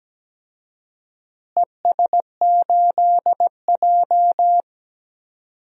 Morse code, heard as a keyed tone, about 17 wpm, 700 Hz. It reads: ES8J
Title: DXCC-100 Call Signs - 4 Characters - Repeated 2x - 17wpm